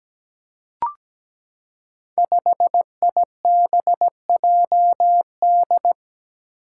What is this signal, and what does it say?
Morse code, heard as a keyed tone, about 17 wpm, 700 Hz. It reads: 5IBJD
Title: Five Challenging Letter-Number Combinations 17wpm